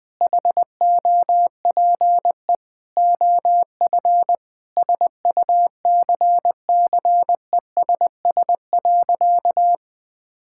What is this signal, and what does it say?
Morse code, heard as a keyed tone, about 21 wpm, 700 Hz. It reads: HOPE OF SUCCESS.
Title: Sentences from Top 1000 Words - Review Part 6 - 20wpm